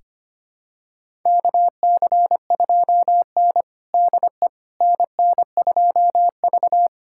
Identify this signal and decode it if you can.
Morse code, heard as a keyed tone, about 25 wpm, 700 Hz. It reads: KC2N DE NN2V
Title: USA Callsign DE Callsign - 4 Characters - 25wpm